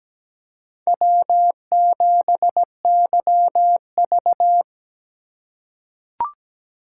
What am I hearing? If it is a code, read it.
Morse code, heard as a keyed tone, about 17 wpm, 700 Hz. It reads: W7YV
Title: USA Call Signs - 4 Characters - 6x Character Spacing - 17wpm